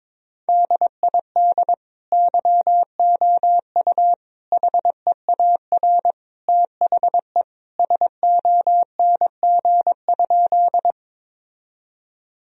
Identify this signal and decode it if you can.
Morse code, heard as a keyed tone, about 22 wpm, 700 Hz. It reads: DID YOU HEAR THE SONG?
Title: Sentences from Top 1000 Words - Review Part 2 - 22wpm